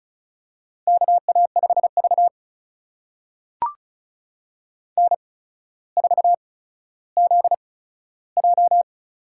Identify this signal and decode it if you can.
Morse code, heard as a keyed tone, about 36 wpm, spaced as for 15 wpm, 700 Hz. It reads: KA5V N4ZJ
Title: USA Call Signs - 4 Characters - 8x Character Spacing - 35wpm